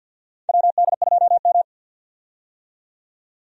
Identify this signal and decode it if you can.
Morse code, heard as a keyed tone, about 49 wpm, 700 Hz. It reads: WD2M
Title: USA Call Signs - 4 Characters - 50wpm